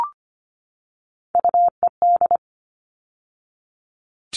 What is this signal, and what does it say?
Morse code, heard as a keyed tone, about 25 wpm, 700 Hz. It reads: UED
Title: All Cut Numbers - 3-Digit 25wpm